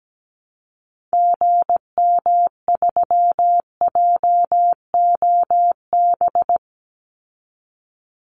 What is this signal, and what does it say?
Morse code, heard as a keyed tone, about 17 wpm, 700 Hz. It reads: GM3JOB
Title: Call Signs Repeated 2x - 17wpm